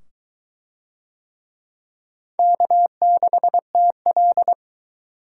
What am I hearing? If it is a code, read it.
Morse code, heard as a keyed tone, about 21 wpm, 700 Hz. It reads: K6TL